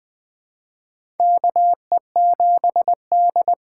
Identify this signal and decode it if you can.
Morse code, heard as a keyed tone, about 20 wpm, 700 Hz. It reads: KE7D
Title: USA Call Signs - 6x Character Spacing - 20wpm